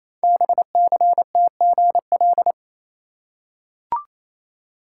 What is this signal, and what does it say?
Morse code, heard as a keyed tone, about 28 wpm, 700 Hz. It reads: BCTGL
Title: Code Groups - 5 Letters - 28wpm